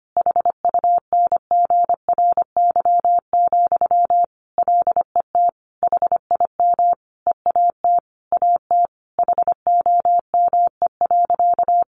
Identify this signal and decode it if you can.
Morse code, heard as a keyed tone, about 25 wpm, 700 Hz. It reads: HUNGRY, LET HIM EAT AT HOME.
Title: Sentences from Top 1400 Words 25wpm